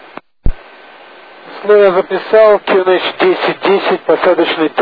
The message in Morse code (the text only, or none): none